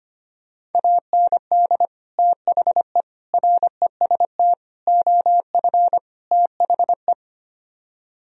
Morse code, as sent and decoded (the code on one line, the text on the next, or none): .- -. -.. / - .... / .-. ... - / --- ..-. / - .... .
AND THE REST OF THE